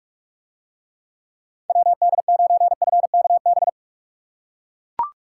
Morse code, heard as a keyed tone, about 41 wpm, 700 Hz. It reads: WD9FKB